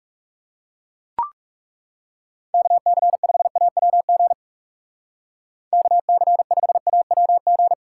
Morse code, heard as a keyed tone, about 38 wpm, 700 Hz. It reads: KC5AWG KC5AWG